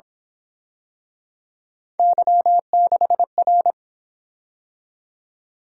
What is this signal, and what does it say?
Morse code, heard as a keyed tone, about 26 wpm, 700 Hz. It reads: Y6R
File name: Speed-Racing - Three Letter-Number Combinations - 22wpm